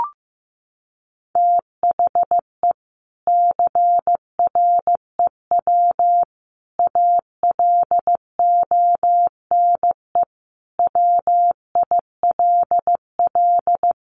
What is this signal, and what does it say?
Morse code, heard as a keyed tone, about 14 wpm, 700 Hz. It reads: THE CREW ALONE WILL
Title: Sentences from Top 1500 Words 15wpm